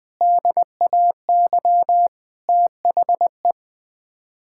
Morse code, heard as a keyed tone, about 20 wpm, 700 Hz. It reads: DAY THE
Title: Sets of 8 Words - 20wpm